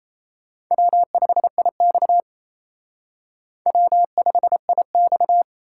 Morse code, heard as a keyed tone, about 34 wpm, 700 Hz. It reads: W5IX W5IX
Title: Speed Racing - USA 1x2 Call Signs 28wpm